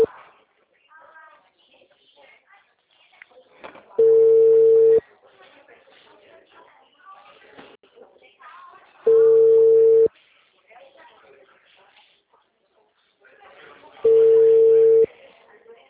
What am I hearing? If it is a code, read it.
Ringback tone (China, 450 Hz), with speech.
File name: out- -2019- - - .99